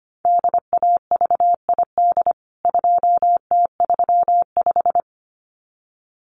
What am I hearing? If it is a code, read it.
Morse code, heard as a keyed tone, about 25 wpm, 700 Hz. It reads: DAVID 2T35
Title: CWT Exchange 25wpm